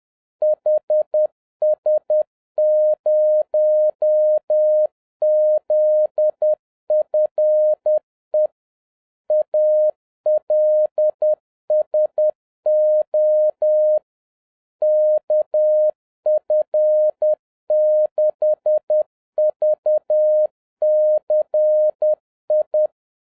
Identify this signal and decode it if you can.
Morse code, heard as a keyed tone, about 10 wpm, 610 Hz. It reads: HS0ZFE ALSO KF6VCI